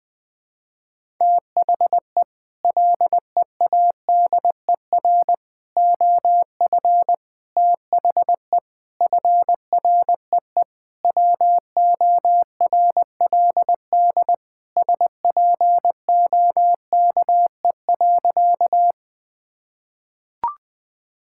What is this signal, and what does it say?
Morse code, heard as a keyed tone, about 20 wpm, 700 Hz. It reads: THE LEADER OF THE FREE WORLD SPOKE.
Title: Sentences with 6-Word Phrases - Spoken Beginning - 20wpm